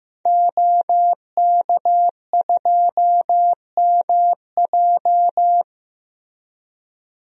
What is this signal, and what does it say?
Morse code, heard as a keyed tone, about 15 wpm, 700 Hz. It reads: OK2MJ
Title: DXCC-100 Call Signs - 5 Characters - 15wpm